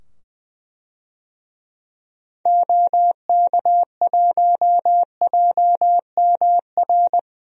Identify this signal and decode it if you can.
Morse code, heard as a keyed tone, about 20 wpm, 700 Hz. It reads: OK1JMR
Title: DXCC-100 Call Signs - 6 Characters - 20wpm